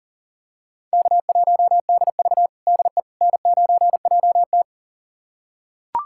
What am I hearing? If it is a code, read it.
Morse code, heard as a keyed tone, about 40 wpm, 700 Hz. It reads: K1DV DE N9JT
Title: USA Callsign DE Callsign - 4 Characters - 40wpm